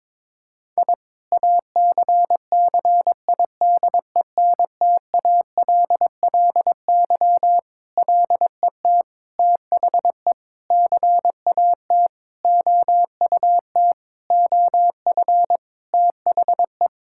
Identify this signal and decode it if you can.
Morse code, heard as a keyed tone, about 22 wpm, 700 Hz. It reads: I ACCIDENTALLY LET THE CAT OUT OF THE
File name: Sentences with 7-Word Phrases - 22wpm